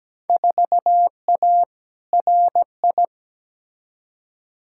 Morse code, heard as a keyed tone, about 17 wpm, 700 Hz. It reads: 4A RI
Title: Field Day Exchanges Encore - 17wpm